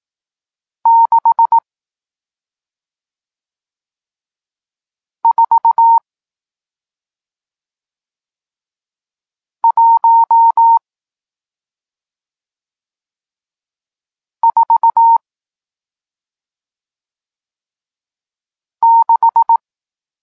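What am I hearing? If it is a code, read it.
Morse code, heard as a keyed tone, about 18 wpm, spaced as for 2 wpm, 920 Hz. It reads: 64146
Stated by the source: Morse code lesson teaching digits 1, 4 and 6, from SP9MOA, Niepołomicki Klub Krótkofalowców (Poland) amateur radio club.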